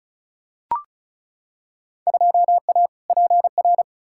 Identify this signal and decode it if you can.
Morse code, heard as a keyed tone, about 34 wpm, 700 Hz. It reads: 2A PR
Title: Field Day Exchanges Encore - 35wpm